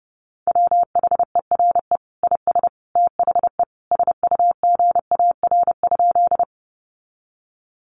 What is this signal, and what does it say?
Morse code, heard as a keyed tone, about 30 wpm, 700 Hz. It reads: WHERE IS THE SUGAR?
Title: Sentences from Top 1200 Words 30wpm